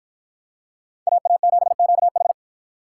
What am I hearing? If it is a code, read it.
Morse code, heard as a keyed tone, about 51 wpm, 700 Hz. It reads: AA7QH